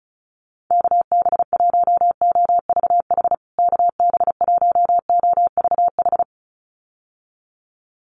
Morse code, heard as a keyed tone, about 34 wpm, 700 Hz. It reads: KB1OVH KB1OVH
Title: USA Call Signs Repeated 2x 35wpm